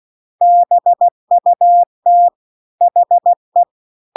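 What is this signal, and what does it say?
Morse code, heard as a keyed tone, about 16 wpm, 700 Hz. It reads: BUT HE
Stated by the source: Morse practice text at 16 WPM.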